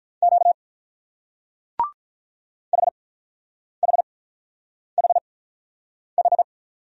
Morse code, heard as a keyed tone, about 43 wpm, spaced as for 9 wpm, 700 Hz. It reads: Q HHHH
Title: Speed-Racing Single Letters 35wpm